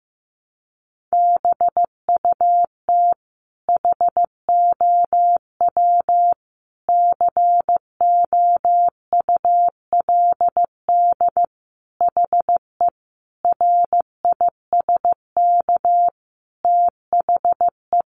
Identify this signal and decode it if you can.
Morse code, heard as a keyed tone, about 15 wpm, 700 Hz. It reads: BUT HOW COULD HE RISK THE
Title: Sentences from Top 600 Words 15wpm